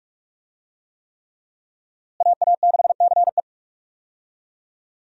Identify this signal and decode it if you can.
Morse code, heard as a keyed tone, about 46 wpm, 700 Hz. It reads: AA6CE
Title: USA Call Signs - 5 Characters - Repeated 3x - 45wpm